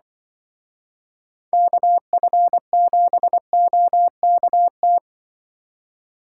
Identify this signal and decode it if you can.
Morse code, heard as a keyed tone, about 24 wpm, 700 Hz. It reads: KF7OKT
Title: Speed Racing - USA Call Signs 20wpm